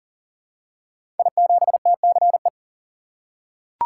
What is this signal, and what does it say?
Morse code, heard as a keyed tone, about 39 wpm, 700 Hz. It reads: I7TCE
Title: Code Groups - 5 Letter:Numbers - 40wpm